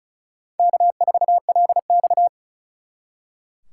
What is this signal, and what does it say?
Morse code, heard as a keyed tone, about 46 wpm, 700 Hz. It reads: K4LX